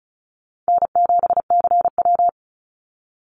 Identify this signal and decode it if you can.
Morse code, heard as a keyed tone, about 36 wpm, 700 Hz. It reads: N7CW
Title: NA SOTA Activator Callsigns 35wpm